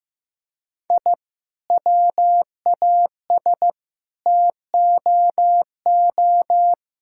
Morse code, heard as a keyed tone, about 15 wpm, 700 Hz. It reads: I WAS TOO
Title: Sentences from Top 1000 Words - Review Part 5 - 15wpm